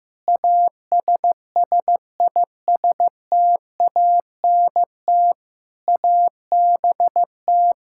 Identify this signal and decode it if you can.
Morse code, heard as a keyed tone, about 15 wpm, 700 Hz. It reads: ASSISTANT ABT
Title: Practice QSO - Exchange 3 - More Info - 15wpm